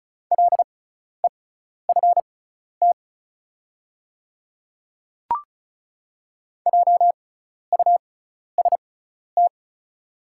Morse code, heard as a keyed tone, about 36 wpm, spaced as for 12 wpm, 690 Hz. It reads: LEFT JUST
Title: 6x Character Spacing - Top 500 Words - 35wpm